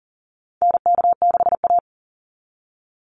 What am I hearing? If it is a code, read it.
Morse code, heard as a keyed tone, about 40 wpm, 700 Hz. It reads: NK6A
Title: NA SOTA Chaser Callsigns 40wpm